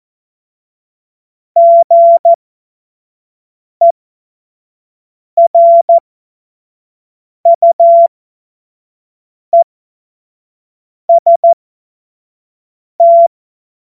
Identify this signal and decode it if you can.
Morse code, heard as a keyed tone, about 14 wpm, spaced as for 5 wpm, 680 Hz. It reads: GERUEST
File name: Morsekurs / Witze / Witz 24.Oktober 2023 25 F70